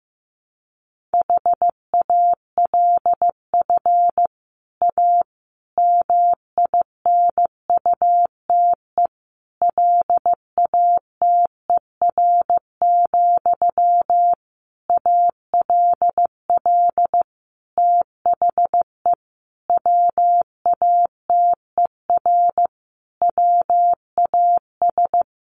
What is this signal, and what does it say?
Morse code, heard as a keyed tone, about 15 wpm, 700 Hz. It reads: HALF A MINUTE LATER, ALL THE WATER WAS
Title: Sentences from Top 1100 Words 15wpm